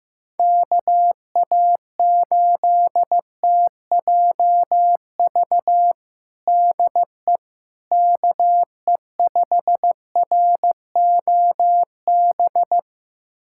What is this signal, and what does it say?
Morse code, heard as a keyed tone, about 15 wpm, 700 Hz. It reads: KA8TJV DE KE5ROB
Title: USA Callsign DE Callsign 15wpm